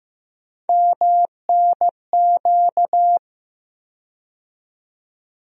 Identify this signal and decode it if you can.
Morse code, heard as a keyed tone, about 15 wpm, 700 Hz. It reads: MNQ
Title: USA Call Sign Postfixes - 3 Letters 15wpm